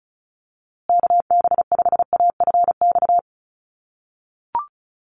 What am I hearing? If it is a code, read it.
Morse code, heard as a keyed tone, about 34 wpm, 700 Hz. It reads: KB5AFX
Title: USA Call Signs Repeated 3x 35wpm